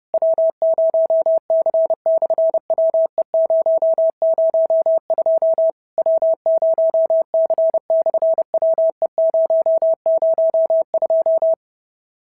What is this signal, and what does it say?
Morse code, heard as a keyed tone, about 30 wpm, 650 Hz. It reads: W0C/WE002 W0C/WE002
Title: SOTA QSO Practice 30wpm